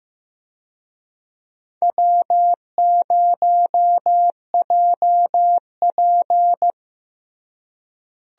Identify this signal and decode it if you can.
Morse code, heard as a keyed tone, about 15 wpm, 700 Hz. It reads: W0JP
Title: USA Call Signs - 4 Characters - Repeated 2x - 15wpm